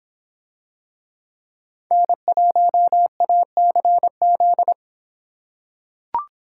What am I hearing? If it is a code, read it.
Morse code, heard as a keyed tone, about 26 wpm, 700 Hz. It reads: N1ACZ